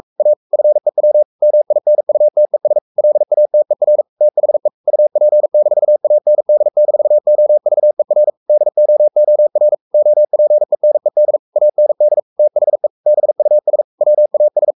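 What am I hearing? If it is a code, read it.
Morse code, heard as a keyed tone, about 44 wpm, 600 Hz. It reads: A FEW MINUTES LATER THE UP-AND-OVER DOOR OPENED AND THE BUS WAS
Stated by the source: Morse code at 43 wpm sending Ken Follett's novel Never in English, from CW med Gnister.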